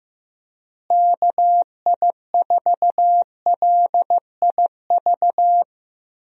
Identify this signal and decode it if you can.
Morse code, heard as a keyed tone, about 16 wpm, 700 Hz. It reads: KI4LIV
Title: USA Call Signs - 6 Characters - Repeated 3x - 15wpm